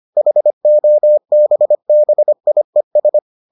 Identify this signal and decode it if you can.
Morse code, heard as a keyed tone, about 25 wpm, 600 Hz. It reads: HOBBIES